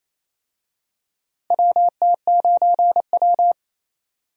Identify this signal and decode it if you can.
Morse code, heard as a keyed tone, about 28 wpm, 700 Hz. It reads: WT9W
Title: USA Call Signs - 4 Characters - Repeated 3x - 28wpm